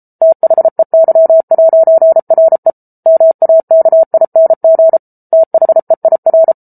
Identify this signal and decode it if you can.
Morse code, heard as a keyed tone, about 34 wpm, 650 Hz. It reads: THEY'RE MAKING THEIR